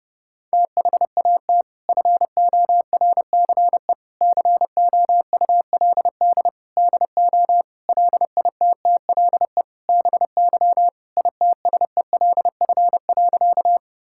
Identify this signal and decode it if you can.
Morse code, heard as a keyed tone, about 30 wpm, 700 Hz. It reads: THAT FORCE COULD DO LITTLE BY ITSELF.
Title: Sentences from Top 1000 Words - Review Part 5 - 30wpm